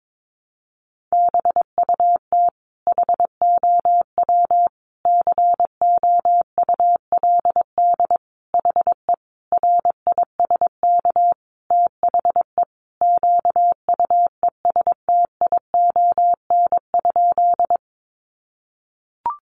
Morse code, heard as a keyed tone, about 22 wpm, 700 Hz. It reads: BUT HOW COULD HE RISK THE QUESTION?
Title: Sentences from Top 600 Words 22wpm